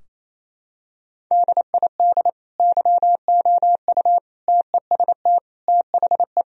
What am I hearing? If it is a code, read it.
Morse code, heard as a keyed tone, about 28 wpm, 700 Hz. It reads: DID YOU TEST THE
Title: Sentences from Top 1000 Words - Review Part 4 - 28wpm